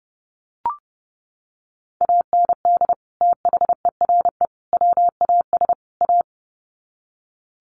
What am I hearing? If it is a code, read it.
Morse code, heard as a keyed tone, about 30 wpm, 700 Hz. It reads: AND THERE WAS A
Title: Sets of 4 Words 30wpm